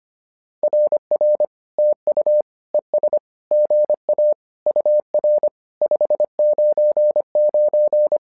Morse code, heard as a keyed tone, about 25 wpm, 600 Hz. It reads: RR TU ES GA UR 599
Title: SOTA QSO Practice 25wpm